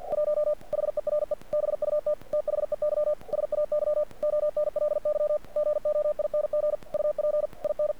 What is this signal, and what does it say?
Morse code, heard as a keyed tone, about 48 wpm, 600 Hz. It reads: 1 HERE BUT THEY SAY ONLY GOING UP IN
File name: archived / CRSnet